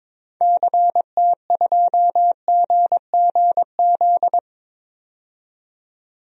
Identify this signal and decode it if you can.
Morse code, heard as a keyed tone, about 21 wpm, 700 Hz. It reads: CT2GGZ
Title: DXCC-100 Call Signs - 6 Characters - 22wpm